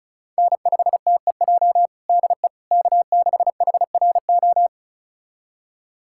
Morse code, heard as a keyed tone, about 36 wpm, 690 Hz. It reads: N5TEJ DE K6HRO
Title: USA Callsign DE Callsign 35wpm